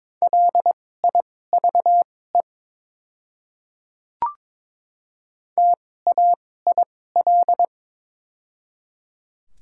Morse code, heard as a keyed tone, about 22 wpm, spaced as for 16 wpm, 700 Hz. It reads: LIVE TAIL
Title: 2x Character Spacing - Top 500 Words - 22wpm